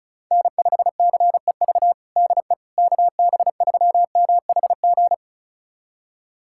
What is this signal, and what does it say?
Morse code, heard as a keyed tone, about 36 wpm, 700 Hz. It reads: N5CEV DE KB3MHG
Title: USA Callsign DE Callsign 35wpm